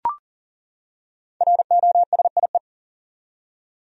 Morse code, heard as a keyed tone, about 40 wpm, 700 Hz. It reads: ROSIE
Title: USA Common Names - Max 5 Letters - 40wpm